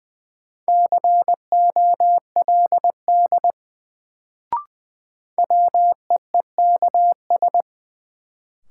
Morse code, heard as a keyed tone, about 20 wpm, 700 Hz. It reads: COLD WEEKS
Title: Sets of 5 Words - New Word Drill - 20wpm